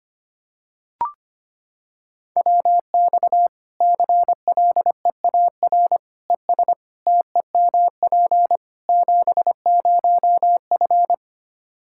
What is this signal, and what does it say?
Morse code, heard as a keyed tone, about 25 wpm, 700 Hz. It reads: WX CLEAR ES TEMP 70F
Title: K1USN SST QSO Practice 25wpm